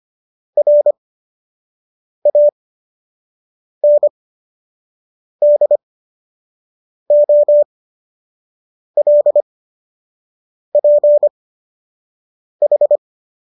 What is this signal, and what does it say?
Morse code, heard as a keyed tone, about 25 wpm, spaced as for 6 wpm, 590 Hz. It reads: RANDOLPH